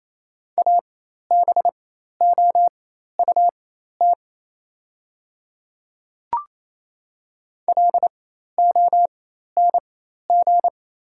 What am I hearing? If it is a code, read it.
Morse code, heard as a keyed tone, about 28 wpm, spaced as for 13 wpm, 700 Hz. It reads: ABOUT LONG
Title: 4x Character Spacing - Top 100 Words - 28wpm